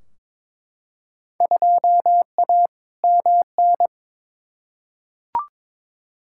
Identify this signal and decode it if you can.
Morse code, heard as a keyed tone, about 22 wpm, 700 Hz. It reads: 2A MN
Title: Field Day Exchanges - 22wpm